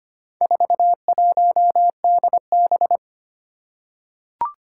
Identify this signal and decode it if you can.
Morse code, heard as a keyed tone, about 25 wpm, 700 Hz. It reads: 41DB
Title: Four Challenging Letter-Number Combinations 25wpm